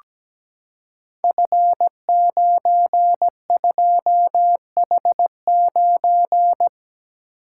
Morse code, heard as a keyed tone, about 17 wpm, 700 Hz. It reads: F92H9
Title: Code Groups - 5 Letter:Numbers - 17wpm